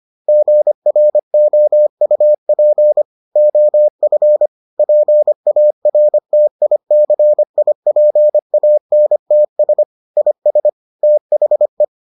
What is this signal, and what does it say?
Morse code, heard as a keyed tone, about 25 wpm, 600 Hz. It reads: GROUP OF PARTICIPANTS IS THE